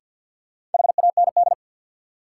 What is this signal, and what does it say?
Morse code, heard as a keyed tone, about 46 wpm, 700 Hz. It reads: HAND
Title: Sets of 3 Words - New Word Drill - 50wpm